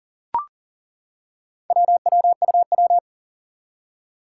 Morse code, heard as a keyed tone, about 38 wpm, 700 Hz. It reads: WWUW
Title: Four Challenging Letter-Number Combinations 40wpm